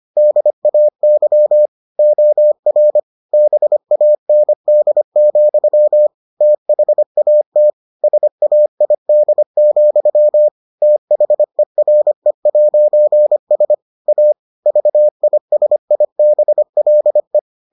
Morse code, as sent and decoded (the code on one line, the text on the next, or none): -.. .- -.-- / --- .-. / -... .- -. -.. --..-- / - .... .- - / ... .- .. -.. --..-- / - .... .-. .----. ... / .- / ...- .. ... .. -... .-.. .
DAY OR BAND, THAT SAID, THERE'S A VISIBLE